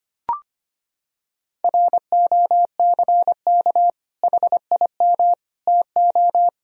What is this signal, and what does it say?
Morse code, heard as a keyed tone, about 25 wpm, 700 Hz. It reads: ROCK HIM TO
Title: Sentences from Top 1000 Words - Review Part 2 - 25wpm